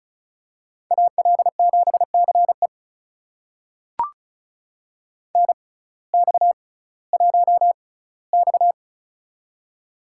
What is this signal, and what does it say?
Morse code, heard as a keyed tone, about 36 wpm, spaced as for 19 wpm, 700 Hz. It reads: AL7CE NX1X
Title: USA Call Signs - 6x Character Spacing - 35wpm